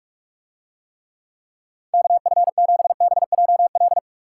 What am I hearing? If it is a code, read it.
Morse code, heard as a keyed tone, about 46 wpm, 700 Hz. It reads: KF7BJL